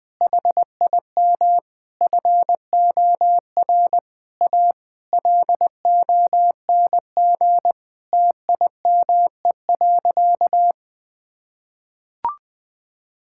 Morse code, heard as a keyed tone, about 20 wpm, 700 Hz. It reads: HIM FOR A LONG TIME.
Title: Sentences from Top 1600 Words 20wpm